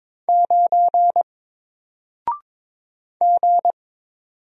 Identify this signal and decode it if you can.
Morse code, heard as a keyed tone, about 22 wpm, 700 Hz. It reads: 9 G
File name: Single-Letter-Number-22wpm